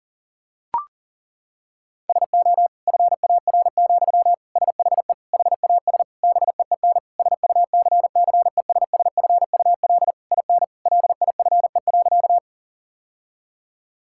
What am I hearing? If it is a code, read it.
Morse code, heard as a keyed tone, about 40 wpm, 700 Hz. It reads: SO FAR, SHE HAS BEEN SUCCESSFUL IN LIFE.
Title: Sentences from Top 1000 Words - Review Part 6 - 40wpm